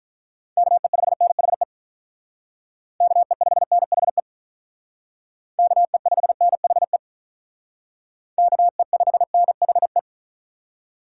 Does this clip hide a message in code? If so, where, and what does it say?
Yes: Morse code from 0.6 to 10.0 s, heard as a keyed tone, about 45 wpm, 700 Hz. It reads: KE5NHE KE5NHE KE5NHE KE5NHE